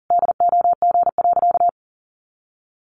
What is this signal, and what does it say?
Morse code, heard as a keyed tone, about 40 wpm, 700 Hz. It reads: DOG.
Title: Additional Easy Phrases 40wpm